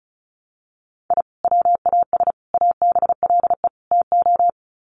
Morse code, heard as a keyed tone, about 34 wpm, 700 Hz. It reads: I WAS ABLE TO
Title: Sets of 4 Words 35wpm